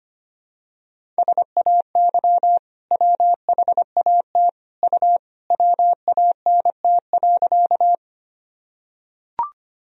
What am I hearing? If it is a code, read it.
Morse code, heard as a keyed tone, about 25 wpm, 700 Hz. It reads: SAY WHAT U WANT.
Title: Sentences from Top 200 Words and Abbreviations 25wpm